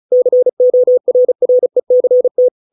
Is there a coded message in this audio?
Yes: Morse code from 0.1 to 2.5 s, heard as a keyed tone, about 34 wpm, 500 Hz. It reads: CORRECT